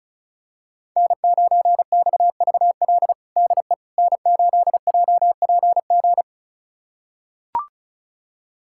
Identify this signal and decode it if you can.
Morse code, heard as a keyed tone, about 36 wpm, 700 Hz. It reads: N9XVL DE N8JPG
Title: USA Callsign DE Callsign - 5 Characters - 35wpm